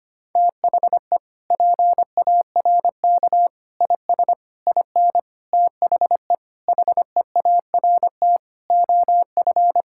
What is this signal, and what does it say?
Morse code, heard as a keyed tone, about 25 wpm, 700 Hz. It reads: THE PARK IS IN THE HEART OF
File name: practice / 25wpm / Sentences with 4-Word Phrases - 25wpm